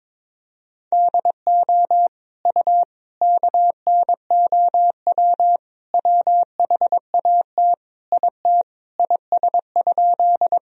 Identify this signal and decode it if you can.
Morse code, heard as a keyed tone, about 21 wpm, 700 Hz. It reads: DO U KNOW WHAT IT IS?
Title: Sentences from Top 100 Words and Abbreviations 22wpm